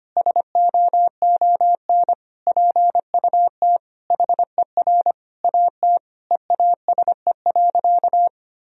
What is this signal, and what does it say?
Morse code, heard as a keyed tone, about 25 wpm, 700 Hz. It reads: SOON PUT HER AT EASE.
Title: Sentences from Top 1000 Words - Review Part 1 - 25wpm